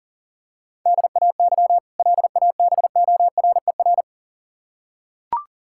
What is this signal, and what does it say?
Morse code, heard as a keyed tone, about 40 wpm, 700 Hz. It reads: DAY LABORER
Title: K1USN SST QSO Practice 40wpm